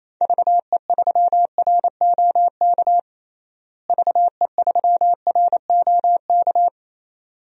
Morse code, heard as a keyed tone, about 27 wpm, 700 Hz. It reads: VE3ROK VE3ROK
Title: DXCC-100 Call Signs - 6 Characters - Repeated 2x - 28wpm